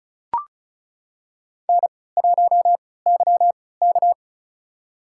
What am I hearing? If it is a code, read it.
Morse code, heard as a keyed tone, about 36 wpm, spaced as for 20 wpm, 700 Hz. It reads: N1YK
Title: USA Call Signs - 4 Characters - 3x Character Spacing - 35wpm